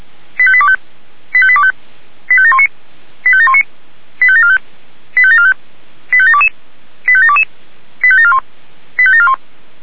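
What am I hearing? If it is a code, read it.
Five-tone selective call (ZVEI), heard as quick runs of five tones, tones of about 70 ms. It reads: 85825 85825 85719 85719 85745 85745 85722 85722 85741 85741
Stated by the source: Five-tone ZVEI selective call 85728 received on SDR.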